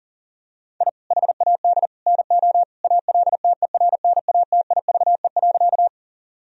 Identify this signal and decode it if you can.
Morse code, heard as a keyed tone, about 37 wpm, 700 Hz. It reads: I HAD NO ALTERNATIVE.
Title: Sentences from Top 1000 Words - Review Part 4 - 40wpm